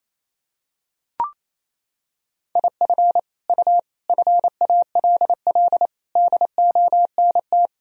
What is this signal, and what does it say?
Morse code, heard as a keyed tone, about 28 wpm, 700 Hz. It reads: IF U FALL DONT
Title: Sentences from Top 500 Words and Abbreviations 28wpm